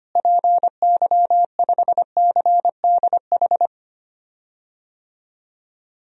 Morse code, heard as a keyed tone, about 25 wpm, 700 Hz. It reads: PY5CDH
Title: DXCC-100 Call Signs - 6 Characters - Repeated 3x - 25wpm